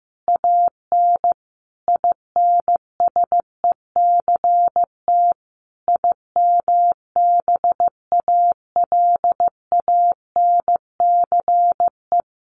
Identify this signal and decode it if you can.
Morse code, heard as a keyed tone, about 15 wpm, 700 Hz. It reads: AN INSECT IMBALANCE
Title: A Sound Of Thunder 15wpm